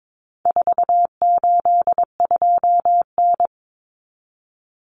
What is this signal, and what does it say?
Morse code, heard as a keyed tone, about 22 wpm, 700 Hz. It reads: 482N
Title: Common Cut Numbers - 4-Digit 22wpm